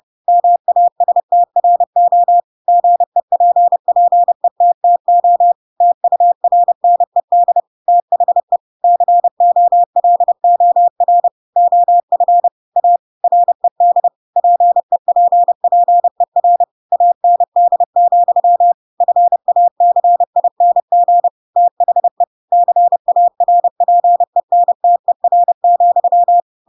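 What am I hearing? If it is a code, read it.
Morse code, heard as a keyed tone, about 30 wpm, 700 Hz. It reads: MASTRO GEPPETTO TURNED THE COLOR OF A RED PEPPER AND, FACING THE CARPENTER,
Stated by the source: Chapter 2 of The Adventures of Pinocchio read as a book in Morse code (CW) at 30 wpm, by ik1hge.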